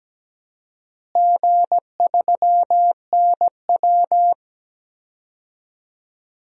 Morse code, heard as a keyed tone, about 17 wpm, 700 Hz. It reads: G3NW
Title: DXCC-100 Call Signs - 4 Characters - Repeated 3x - 17wpm